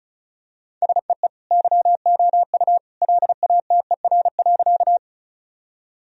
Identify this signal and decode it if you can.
Morse code, heard as a keyed tone, about 36 wpm, 700 Hz. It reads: SEE YOU LATER.
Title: Sentences from Top 1000 Words - Review Part 4 - 35wpm